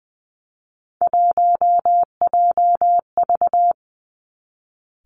Morse code, heard as a keyed tone, about 20 wpm, 700 Hz. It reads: 1JV
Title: Three Challenging Letter-Number Combinations 20wpm